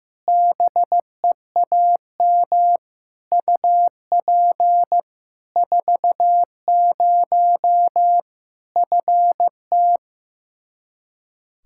Morse code, heard as a keyed tone, about 16 wpm, 700 Hz. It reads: BEAM UP 40 FT
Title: Practice QSO - Exchange 2 - Info - 15wpm